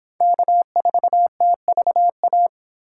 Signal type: Morse code

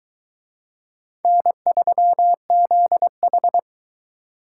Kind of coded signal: Morse code

